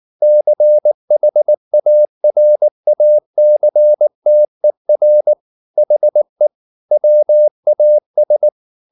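Morse code, heard as a keyed tone, about 19 wpm, 600 Hz. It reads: CHARACTER HE WAS
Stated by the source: Morse code at 19 wpm from CW med Gnister, reading Ken Follett's novel Never in English.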